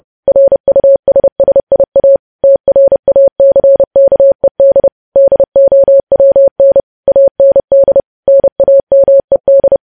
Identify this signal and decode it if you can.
Morse code, heard as a keyed tone, about 30 wpm, 550 Hz. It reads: RUSSIA TRACKED DOWN AND NAMED
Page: news headlines in morse code at 30 wpm